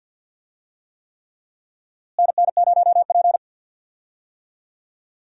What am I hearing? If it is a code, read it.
Morse code, heard as a keyed tone, about 47 wpm, 700 Hz. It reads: NN0P
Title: USA Call Signs - 4 Characters - Repeated 3x - 50wpm